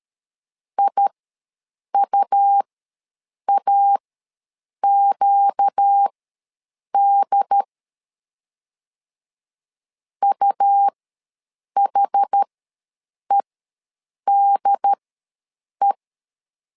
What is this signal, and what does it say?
Morse code, heard as a keyed tone, about 13 wpm, spaced as for 7 wpm, 800 Hz. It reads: IUAQD UHEDE
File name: LEZIONI / 4 Q ws 12wpm 1912